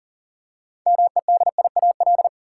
Morse code, heard as a keyed tone, about 37 wpm, 700 Hz. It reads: MEDIAL
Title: Practice QSO - Exchange 3 - More Info - 40wpm